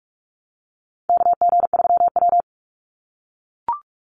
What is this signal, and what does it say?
Morse code, heard as a keyed tone, about 44 wpm, 700 Hz. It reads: KG3W
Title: NA SOTA Chaser Callsigns 45wpm